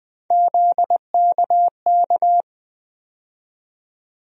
Morse code, heard as a keyed tone, about 20 wpm, 700 Hz. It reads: ZKK
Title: USA Call Sign Postfixes - 3 Letters 20wpm